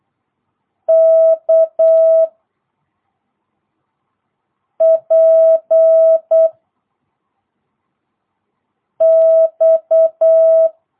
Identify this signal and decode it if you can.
Morse code, heard as a keyed tone, about 8 wpm, 660 Hz. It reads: K P X